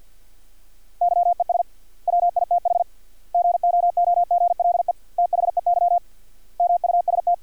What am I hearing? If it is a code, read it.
Morse code, heard as a keyed tone, about 50 wpm, 700 Hz. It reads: YES WITH GOOGLE THEY MUST
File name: archived / CRSnet